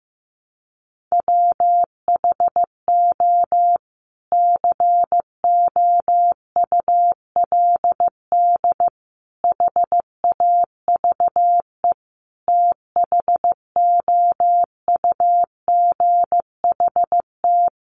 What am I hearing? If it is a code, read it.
Morse code, heard as a keyed tone, about 15 wpm, 700 Hz. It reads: WHO COULD HAVE THOUGHT
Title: Sentences from Top 300 Words 15wpm